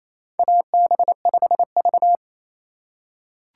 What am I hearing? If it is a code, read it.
Morse code, heard as a keyed tone, about 27 wpm, 700 Hz. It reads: AB5V
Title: USA Call Signs - 4 Characters - 4x Character Spacing - 28wpm